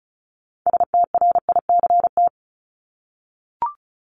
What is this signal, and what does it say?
Morse code, heard as a keyed tone, about 34 wpm, 700 Hz. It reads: STRICT
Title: Top 1501-1750 Words 35wpm